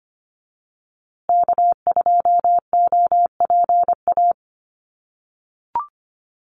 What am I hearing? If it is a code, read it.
Morse code, heard as a keyed tone, about 25 wpm, 700 Hz. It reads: K2OPA
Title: USA Call Signs Repeated 3x 25wpm